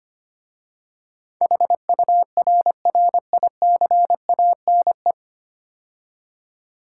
Morse code, heard as a keyed tone, about 25 wpm, 700 Hz. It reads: HURRICANE
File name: 9 Letter Words - First 3 Letters Given - 25wpm